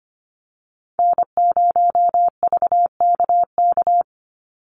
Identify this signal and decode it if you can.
Morse code, heard as a keyed tone, about 25 wpm, 700 Hz. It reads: N0VKK
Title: USA Call Signs 25wpm